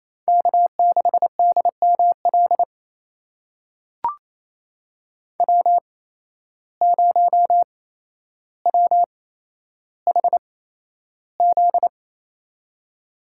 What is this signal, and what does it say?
Morse code, heard as a keyed tone, about 28 wpm, spaced as for 12 wpm, 700 Hz. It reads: K6DML W0WHZ
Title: USA Call Signs - 8x Character Spacing - 28wpm